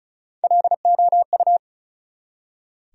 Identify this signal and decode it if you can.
Morse code, heard as a keyed tone, about 36 wpm, 700 Hz. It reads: LOU
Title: USA Common Names - Max 4 Letters - 35wpm